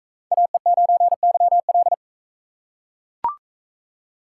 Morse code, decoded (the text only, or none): AE9YL